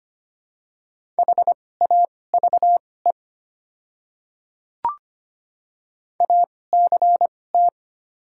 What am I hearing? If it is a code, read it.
Morse code, heard as a keyed tone, about 25 wpm, spaced as for 18 wpm, 700 Hz. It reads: HAVE ACT
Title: 2x Character Spacing - Top 300 Words - 25wpm